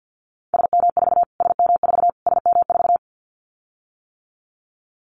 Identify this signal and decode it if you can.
Morse code, heard as a keyed tone, about 50 wpm, 700 Hz. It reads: SM4 SM4 SM4
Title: Callsign Prefixes for Top 25 DXCC - Repeated 3x - 50wpm